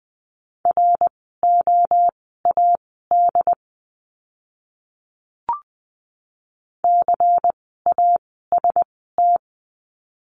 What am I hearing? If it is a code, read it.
Morse code, heard as a keyed tone, about 20 wpm, spaced as for 14 wpm, 700 Hz. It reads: ROAD CAST
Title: 094 Lesson 16f - Letter c - Words 20wpm